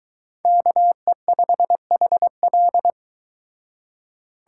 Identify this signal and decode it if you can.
Morse code, heard as a keyed tone, about 23 wpm, 700 Hz. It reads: KE5HL